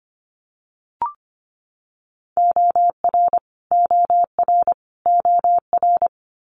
Morse code, heard as a keyed tone, about 25 wpm, 700 Hz. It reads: OR OR OR
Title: Multi-letter Phonemes Repeated 3x 25wpm